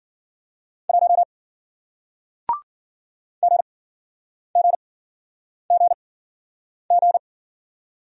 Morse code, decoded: 1 GGGG

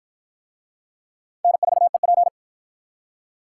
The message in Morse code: N4EP

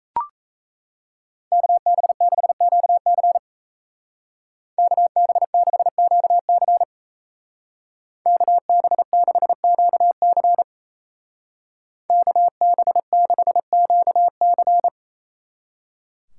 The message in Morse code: KB6QC KB6QC KB6QC KB6QC

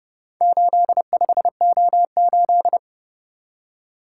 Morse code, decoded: 85O8

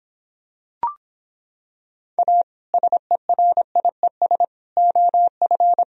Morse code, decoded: A SERIES OF

